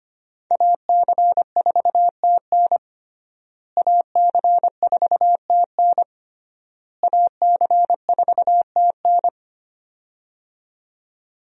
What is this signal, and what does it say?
Morse code, heard as a keyed tone, about 25 wpm, 700 Hz. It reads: AC4TN AC4TN AC4TN
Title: USA Call Signs - 5 Characters - Repeated 3x - 25wpm